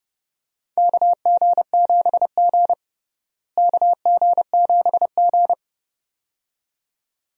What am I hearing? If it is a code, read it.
Morse code, heard as a keyed tone, about 30 wpm, 700 Hz. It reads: KG7G KG7G
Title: USA Call Signs - 4 Characters - Repeated 2x - 30wpm